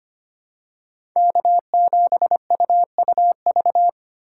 Morse code, heard as a keyed tone, about 25 wpm, 700 Hz. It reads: K7UUV
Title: USA Call Signs - 2x Character Spacing - 25wpm